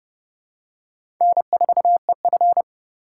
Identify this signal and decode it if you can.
Morse code, heard as a keyed tone, about 30 wpm, 700 Hz. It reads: N4EF
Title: USA Call Signs - 4 Characters - 4x Character Spacing - 30wpm